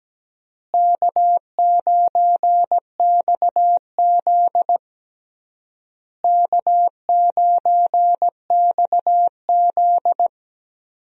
Morse code, heard as a keyed tone, about 17 wpm, 700 Hz. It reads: K9XZ K9XZ
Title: USA Call Signs - 4 Characters - Repeated 3x - 17wpm